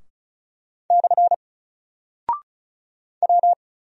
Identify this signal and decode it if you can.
Morse code, heard as a keyed tone, about 36 wpm, 700 Hz. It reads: / W
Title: Single Character (Letter:Number:Punctuation:Prosign) Alt 35wpm